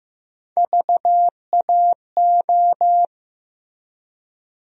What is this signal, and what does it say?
Morse code, heard as a keyed tone, about 15 wpm, 700 Hz. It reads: VAO
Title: USA Call Sign Postfixes - 3 Letters 15wpm